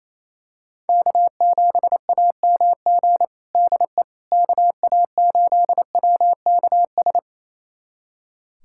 Morse code, heard as a keyed tone, about 28 wpm, 700 Hz. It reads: K7AMG DE KA8WKS
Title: USA Callsign DE Callsign 28wpm